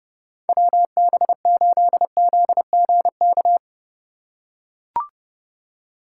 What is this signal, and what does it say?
Morse code, heard as a keyed tone, about 29 wpm, 700 Hz. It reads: WB8ZGK